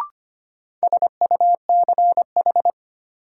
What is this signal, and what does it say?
Morse code, heard as a keyed tone, about 25 wpm, 700 Hz. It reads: SUCH
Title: Sentences from Top 200 Words - New Word Drill - 25wpm